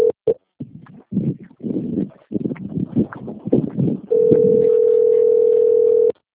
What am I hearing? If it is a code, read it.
Ringback tone (North America), 440 + 480 Hz.
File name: rg-1600-unknown- - - .39564